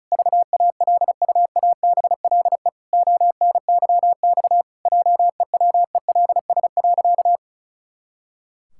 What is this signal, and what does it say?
Morse code, heard as a keyed tone, about 34 wpm, 700 Hz. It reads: VALUABLE ONYX JEWELS.
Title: Pangram Challenge v2 - 35wpm